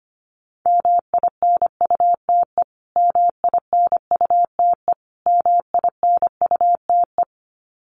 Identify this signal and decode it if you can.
Morse code, heard as a keyed tone, about 25 wpm, 700 Hz. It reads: MINUTE MINUTE MINUTE
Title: Top 500 Words Repeated 25wpm